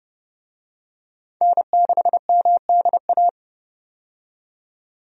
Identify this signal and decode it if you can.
Morse code, heard as a keyed tone, about 31 wpm, 700 Hz. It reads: N6MDA
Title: USA Call Signs - 5 Characters - 30wpm